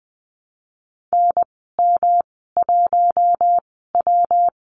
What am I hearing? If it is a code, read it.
Morse code, heard as a keyed tone, about 20 wpm, spaced as for 15 wpm, 700 Hz. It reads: NM1W
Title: 167 Lesson 27g - Number 7 - Call Signs 20wpm